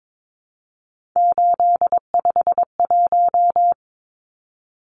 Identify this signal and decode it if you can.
Morse code, heard as a keyed tone, about 22 wpm, 700 Hz. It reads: 851
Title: Numbers - 3-Digit 22wpm